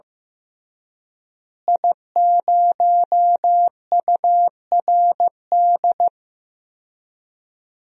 Morse code, heard as a keyed tone, about 15 wpm, 700 Hz. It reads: I0URD
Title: DXCC-100 Call Signs - 5 Characters - Repeated 3x - 15wpm